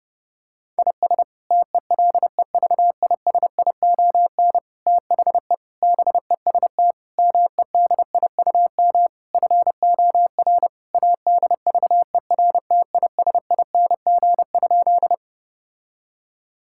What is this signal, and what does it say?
Morse code, heard as a keyed tone, about 30 wpm, 700 Hz. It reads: IS TELEVISION THE BEST MEDIUM FOR ADVERTISING?
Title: Sentences from Top 1000 Words - Review Part 6 - 30wpm